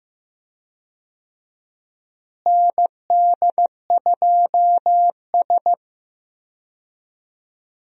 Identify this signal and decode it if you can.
Morse code, heard as a keyed tone, about 15 wpm, 700 Hz. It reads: ND2S
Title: USA Call Signs - 4 Characters - Repeated 3x - 15wpm